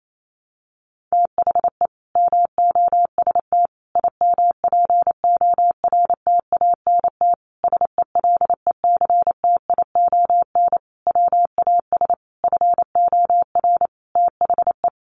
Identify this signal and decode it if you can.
Morse code, heard as a keyed tone, about 28 wpm, 700 Hz. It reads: THE MOST IMPORTANT SELECTION WAS FOR THE
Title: Sentences from Top 1400 Words 28wpm